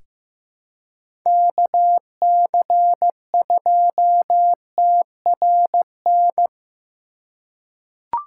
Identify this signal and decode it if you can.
Morse code, heard as a keyed tone, about 15 wpm, 700 Hz. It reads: KC2TRN
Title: USA Call Signs - 6 Characters - 2x Character Spacing - 15wpm